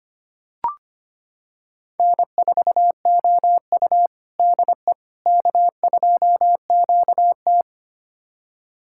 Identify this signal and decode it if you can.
Morse code, heard as a keyed tone, about 25 wpm, 700 Hz. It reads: N4OU DE K2QT
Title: USA Callsign DE Callsign - 4 Characters - 25wpm